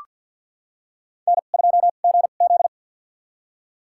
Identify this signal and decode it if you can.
Morse code, heard as a keyed tone, about 47 wpm, spaced as for 35 wpm, 700 Hz. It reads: N2GZ
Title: USA Call Signs - 4 Characters - 2x Character Spacing - 50wpm